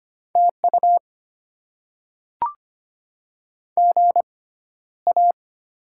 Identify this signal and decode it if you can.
Morse code, heard as a keyed tone, about 25 wpm, spaced as for 13 wpm, 700 Hz. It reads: TU GA
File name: Top 20 QSO Elements - 6x Character Spacing - 25wpm